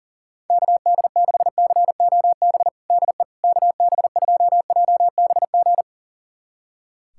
Morse code, heard as a keyed tone, about 40 wpm, 700 Hz. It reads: KD6COB DE KB2JBG
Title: USA Callsign DE Callsign - 6 Characters - 40wpm